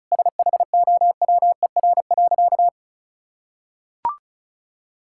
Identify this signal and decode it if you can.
Morse code, heard as a keyed tone, about 36 wpm, 700 Hz. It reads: SHOWER.
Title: Sentences from Top 1600 Words 35wpm